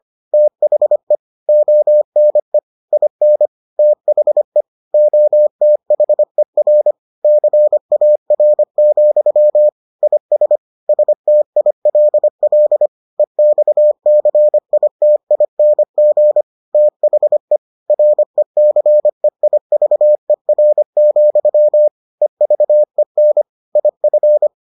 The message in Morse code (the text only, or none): THE ONE IN THE OTHER CAR, IS STILL EXCITING THE RECEIVER, EVEN IF